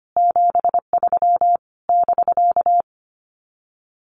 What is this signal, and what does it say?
Morse code, heard as a keyed tone, about 25 wpm, 700 Hz. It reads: 73 <BK>
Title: Common QSO Phrases Encore 25wpm